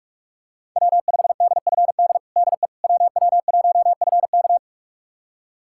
Morse code, heard as a keyed tone, about 41 wpm, 700 Hz. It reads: W5DFD DE WW1FK
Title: USA Callsign DE Callsign - 5 Characters - 45wpm